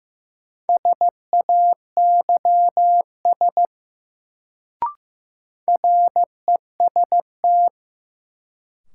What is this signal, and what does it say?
Morse code, heard as a keyed tone, about 16 wpm, 700 Hz. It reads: SAYS REST
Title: Sets of 2 Words - New Word Drill - 15wpm